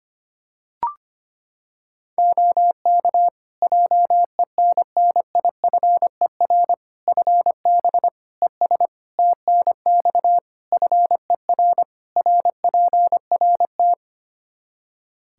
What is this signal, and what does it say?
Morse code, heard as a keyed tone, about 25 wpm, 700 Hz. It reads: OK JENNIFER FB ES TNX FER RPRT
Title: Practice QSO - Exchange 2 - Info - 25wpm